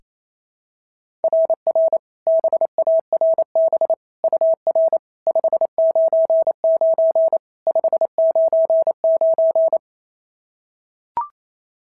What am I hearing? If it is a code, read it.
Morse code, heard as a keyed tone, about 28 wpm, 690 Hz. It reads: RR BARB UR 599 599
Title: SOTA QSO Practice 28wpm